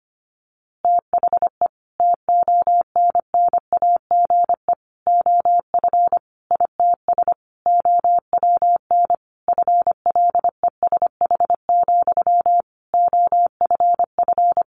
Morse code, heard as a keyed tone, about 25 wpm, 700 Hz. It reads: THE TONNAGE OF ITS OWN FLESH, OFF
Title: A Sound Of Thunder 25wpm